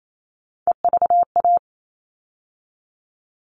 Morse code, heard as a keyed tone, about 28 wpm, 700 Hz. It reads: EVA
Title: All Cut Numbers - 3-Digit 28wpm